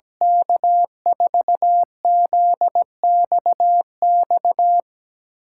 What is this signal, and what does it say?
Morse code, heard as a keyed tone, about 17 wpm, 700 Hz. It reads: K4ZXX
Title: USA Call Signs - 6 Characters - 6x Character Spacing - 17wpm